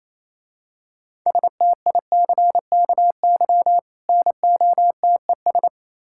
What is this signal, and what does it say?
Morse code, heard as a keyed tone, about 27 wpm, 700 Hz. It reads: STICKY NOTES
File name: Easy Words - Speed-Racing with Hints - 22wpm